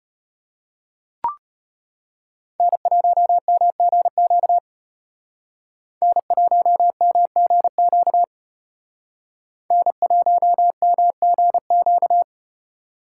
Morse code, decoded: N1MGQ N1MGQ N1MGQ

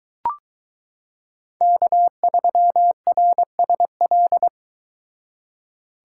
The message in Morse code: K3RSL

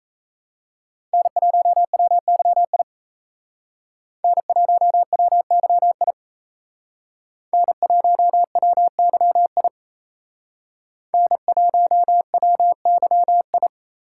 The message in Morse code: N1WYI N1WYI N1WYI N1WYI

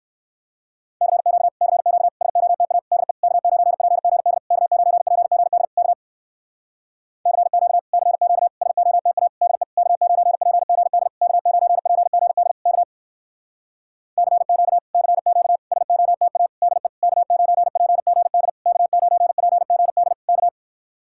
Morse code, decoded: CQ CQ SOTA DE K9PGD K9PGD K CQ CQ SOTA DE K9PGD K9PGD K CQ CQ SOTA DE K9PGD K9PGD K

